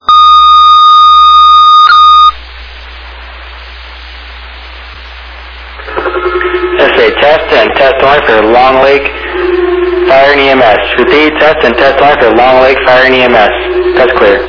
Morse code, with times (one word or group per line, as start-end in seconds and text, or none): none